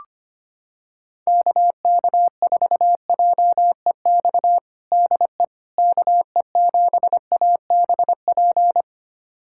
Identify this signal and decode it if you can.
Morse code, heard as a keyed tone, about 25 wpm, 700 Hz. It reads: KK4JEX DE KE7ABP
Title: USA Callsign DE Callsign - 6 Characters - 25wpm